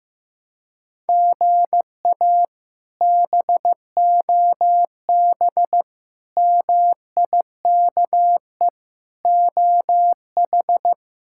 Morse code, heard as a keyed tone, about 15 wpm, 700 Hz. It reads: GA BOB MIKE OH
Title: K1USN SST QSO Practice - 15wpm